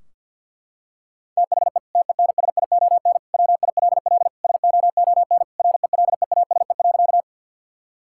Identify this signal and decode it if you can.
Morse code, heard as a keyed tone, about 50 wpm, 700 Hz. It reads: THE TENSION WILL SOON RELEASE.
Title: Sentences from Top 1600 Words 50wpm